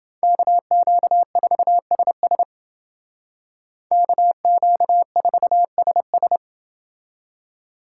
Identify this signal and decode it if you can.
Morse code, heard as a keyed tone, about 29 wpm, 700 Hz. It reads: KQ4SS KQ4SS